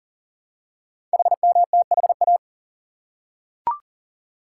Morse code, heard as a keyed tone, about 42 wpm, 690 Hz. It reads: HMTHA
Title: Code Groups - 5 Letters - 40wpm